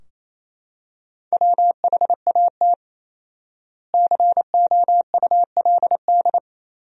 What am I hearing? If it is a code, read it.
Morse code, heard as a keyed tone, about 28 wpm, 700 Hz. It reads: WHAT COULD
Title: 4x Word-Spacing - Sentences from Top 400 Words - 28wpm